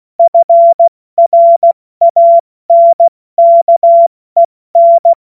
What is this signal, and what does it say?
Morse code, heard as a keyed tone, about 16 wpm, 680 Hz. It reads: FRANKEN
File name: Morsekurs / Witze / Witz 16. Mai 2023 70 F80